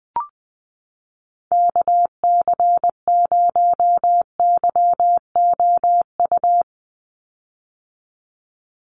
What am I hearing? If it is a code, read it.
Morse code, heard as a keyed tone, about 20 wpm, 700 Hz. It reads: KC0YOU
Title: USA Call Signs 20wpm